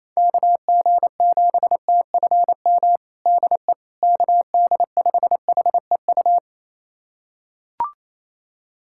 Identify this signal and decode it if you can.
Morse code, heard as a keyed tone, about 28 wpm, 700 Hz. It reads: KG7TFM DE KD5HEU
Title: USA Callsign DE Callsign - 6 Characters - 28wpm